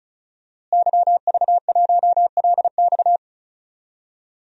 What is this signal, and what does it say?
Morse code, heard as a keyed tone, about 36 wpm, 700 Hz. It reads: YV1LX
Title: DXCC-100 Call Signs - 5 Characters - Repeated 2x - 35wpm